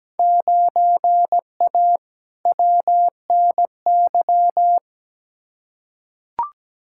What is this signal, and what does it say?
Morse code, heard as a keyed tone, about 17 wpm, 700 Hz. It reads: 9A WNY
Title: Field Day Exchanges Encore - 17wpm